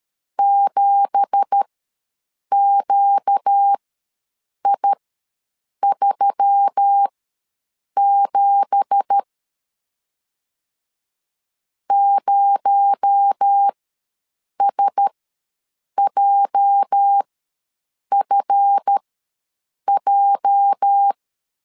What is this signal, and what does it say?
Morse code, heard as a keyed tone, about 12 wpm, spaced as for 7 wpm, 800 Hz. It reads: 7QI37 0SJFJ
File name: LEZIONI / 9 2867 ws 12wpm 3431